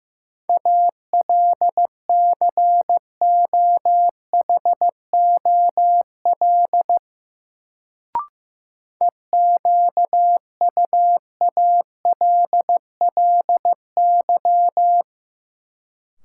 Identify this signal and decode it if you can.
Morse code, heard as a keyed tone, about 15 wpm, 700 Hz. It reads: ALCOHOL EQUALLY
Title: Sentences from Top 1400 Words - New Word Drill Alt - 15wpm